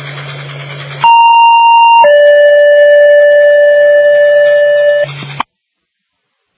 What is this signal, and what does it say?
Two-tone page, heard as a one-second tone, then a three-second tone, about 930 then 580 Hz. It is heard from 1.0 to 5.0 s.